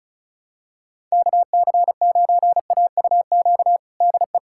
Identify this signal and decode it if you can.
Morse code, heard as a keyed tone, about 36 wpm, 700 Hz. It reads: KC9AUQ DE
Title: USA Callsign DE Callsign - 6 Characters - 35wpm